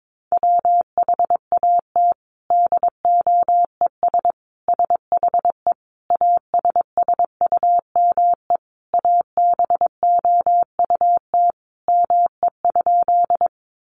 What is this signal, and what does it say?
Morse code, heard as a keyed tone, about 22 wpm, 700 Hz. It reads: WHAT DOES SHE ASSUME ABOUT ME?
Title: Sentences from Top 1500 Words 22wpm